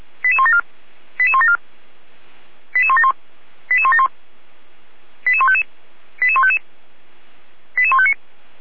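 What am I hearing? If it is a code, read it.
Five-tone selective call (ZVEI), heard as quick runs of five tones, tones of about 70 ms. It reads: 88174 88174 88171 88171 88155 88155 88159
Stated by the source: ZVEI five-tone call 88150 received by SDR.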